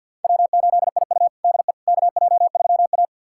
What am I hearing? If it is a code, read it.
Morse code, heard as a keyed tone, about 52 wpm, 700 Hz. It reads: W8IU DE KJ3A
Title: USA Callsign DE Callsign - 4 Characters - 50wpm